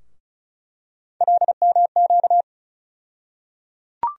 Morse code, heard as a keyed tone, about 34 wpm, 700 Hz. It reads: LMQ
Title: USA Call Sign Postfixes - 3 Letters 35wpm